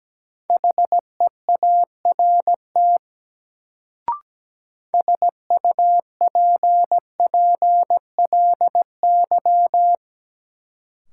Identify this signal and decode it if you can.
Morse code, heard as a keyed tone, about 17 wpm, 700 Hz. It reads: HEART SUPPLY
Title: Sentences from Top 1200 Words - New Word Drill Alt - 17wpm